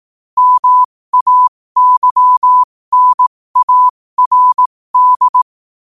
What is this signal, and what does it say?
Morse code, heard as a keyed tone, about 18 wpm, spaced as for 15 wpm, 1000 Hz. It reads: MAYNARD
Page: New quotes every day in morse code at 15 Words per minute.